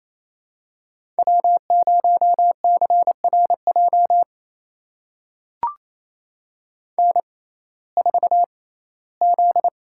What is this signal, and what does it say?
Morse code, heard as a keyed tone, about 28 wpm, spaced as for 17 wpm, 700 Hz. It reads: W0CRJ N4Z
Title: USA Call Signs - 5 Characters - 6x Character Spacing - 28wpm